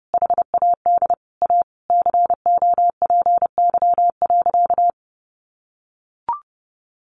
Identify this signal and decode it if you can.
Morse code, heard as a keyed tone, about 30 wpm, 700 Hz. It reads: HAD A COPY.
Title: Sentences from Top 1400 Words 30wpm